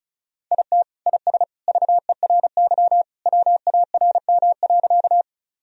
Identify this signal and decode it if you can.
Morse code, heard as a keyed tone, about 36 wpm, 700 Hz. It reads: IT IS VERY WARM.
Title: Sentences from Top 1000 Words - Review Part 2 - 35wpm